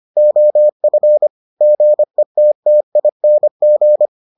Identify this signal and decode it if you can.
Morse code, heard as a keyed tone, about 25 wpm, 600 Hz. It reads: OF GETTING